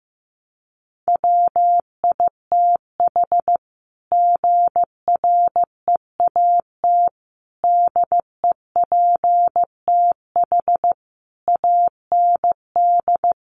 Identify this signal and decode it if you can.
Morse code, heard as a keyed tone, about 15 wpm, 700 Hz. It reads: WITH GREAT DEPTH AND
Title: Sentences from Top 1200 Words 15wpm